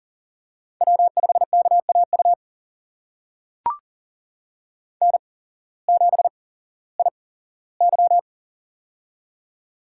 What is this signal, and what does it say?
Morse code, heard as a keyed tone, about 40 wpm, spaced as for 19 wpm, 700 Hz. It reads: W5KAU N7IY
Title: USA Call Signs - 8x Character Spacing - 40wpm